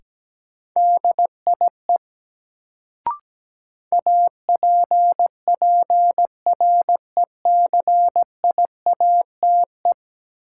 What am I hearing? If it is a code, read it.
Morse code, heard as a keyed tone, about 17 wpm, 700 Hz. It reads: DIE APPRECIATE
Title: Sentences from Top 1300 Words - New Word Drill - 17wpm